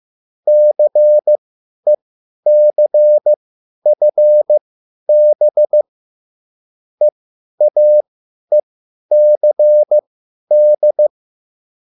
Morse code, heard as a keyed tone, about 15 wpm, spaced as for 10 wpm, 600 Hz. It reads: CECFB EAECD
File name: cwbook